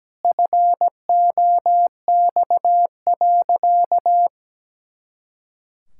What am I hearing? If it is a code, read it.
Morse code, heard as a keyed tone, about 17 wpm, 700 Hz. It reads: FOX.
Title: Pangram Challenge v2 - 17wpm